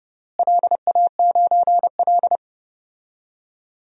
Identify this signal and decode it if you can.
Morse code, heard as a keyed tone, about 30 wpm, 700 Hz. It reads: LA9L
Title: DXCC-100 Call Signs - 4 Characters - Repeated 2x - 30wpm